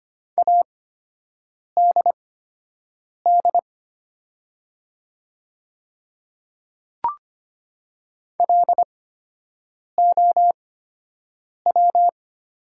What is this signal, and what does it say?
Morse code, heard as a keyed tone, about 25 wpm, spaced as for 7 wpm, 700 Hz. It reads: ADD LOW
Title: 8x Character Spacing - Top 300 Words - 25wpm